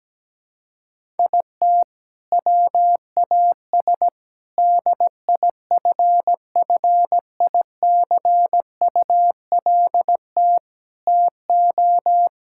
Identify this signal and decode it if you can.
Morse code, heard as a keyed tone, about 17 wpm, 700 Hz. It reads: IT WAS DIFFICULT TO
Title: Sentences from Top 1600 Words 17wpm